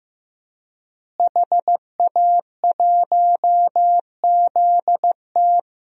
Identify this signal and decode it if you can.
Morse code, heard as a keyed tone, about 16 wpm, 700 Hz. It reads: HA1ZT
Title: DXCC-100 Call Signs - 5 Characters - Repeated 2x - 15wpm